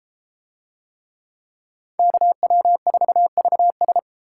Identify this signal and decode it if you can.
Morse code, heard as a keyed tone, about 32 wpm, 700 Hz. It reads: KW4VS